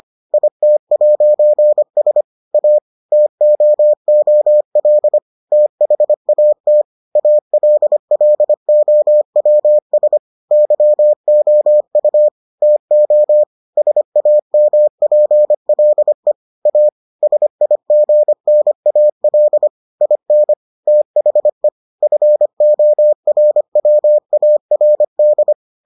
Morse code, heard as a keyed tone, about 25 wpm, 600 Hz. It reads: IT'S A TOOL THAT ALLOWS YOU TO SAMPLE A SIGNAL IN THE FORWARD